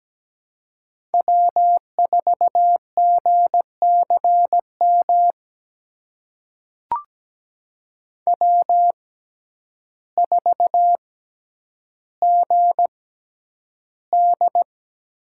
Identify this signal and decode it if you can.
Morse code, heard as a keyed tone, about 17 wpm, spaced as for 9 wpm, 700 Hz. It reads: W4GCM W4GD
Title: USA Call Signs - 5 Characters - 6x Character Spacing - 17wpm